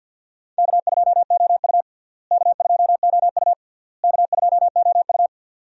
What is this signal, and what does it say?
Morse code, heard as a keyed tone, about 47 wpm, 700 Hz. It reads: K2OU K2OU K2OU
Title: USA Call Signs - 4 Characters - Repeated 3x - 50wpm